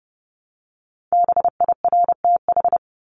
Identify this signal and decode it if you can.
Morse code, heard as a keyed tone, about 30 wpm, 700 Hz. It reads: BIRTH
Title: Top 751-1000 Words 30wpm